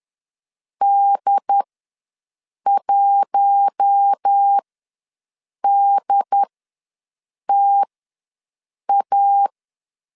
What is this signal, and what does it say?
Morse code, heard as a keyed tone, about 10 wpm, spaced as for 6 wpm, 800 Hz. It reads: D1DTA
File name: LEZIONI / 5 LBVC / 5 LB ws 10wpm 3616